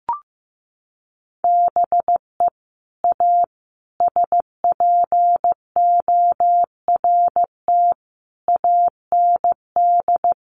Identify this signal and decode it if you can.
Morse code, heard as a keyed tone, about 15 wpm, 700 Hz. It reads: BE A SPORT AND
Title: Sentences from Top 1500 Words 15wpm